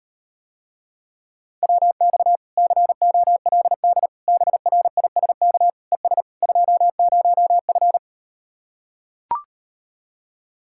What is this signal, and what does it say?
Morse code, heard as a keyed tone, about 38 wpm, 700 Hz. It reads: WX COLD BRISK ES 20F